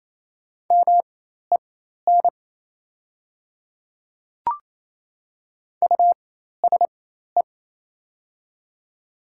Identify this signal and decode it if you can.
Morse code, heard as a keyed tone, about 28 wpm, spaced as for 13 wpm, 700 Hz. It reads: MEN USE
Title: 4x Character Spacing - Top 200 Words - 28wpm